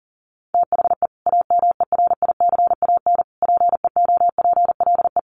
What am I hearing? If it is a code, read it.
Morse code, heard as a keyed tone, about 37 wpm, 700 Hz. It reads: THE AMERICAN PEOPLE
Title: Sets of 3 Words 40wpm